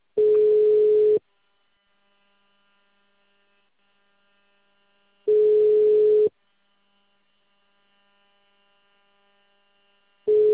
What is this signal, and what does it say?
Ringback tone (Europe, 425 Hz).
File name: out- -2044- - - .24974